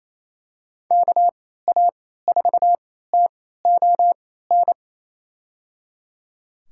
Morse code, heard as a keyed tone, about 28 wpm, spaced as for 16 wpm, 700 Hz. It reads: KA4TON
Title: USA Call Signs - 6 Characters - 3x Character Spacing - 28wpm